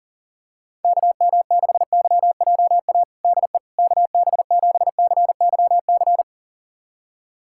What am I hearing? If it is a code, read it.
Morse code, heard as a keyed tone, about 40 wpm, 700 Hz. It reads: KM6YJA DE KB7CYC
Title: USA Callsign DE Callsign - 6 Characters - 40wpm